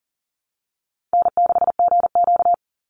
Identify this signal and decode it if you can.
Morse code, heard as a keyed tone, about 40 wpm, 700 Hz. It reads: N6GQ
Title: Easy Call Signs Repeated 3x - 40wpm